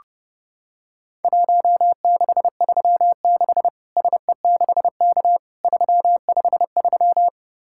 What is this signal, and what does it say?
Morse code, heard as a keyed tone, about 32 wpm, 700 Hz. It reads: 1636 SE6K 353
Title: Worked All Europe DX Contest - QTC Exchange - 30wpm